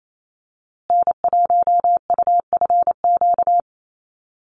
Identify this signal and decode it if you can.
Morse code, heard as a keyed tone, about 28 wpm, 690 Hz. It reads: N1UFQ
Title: USA Call Signs Repeated 3x 28wpm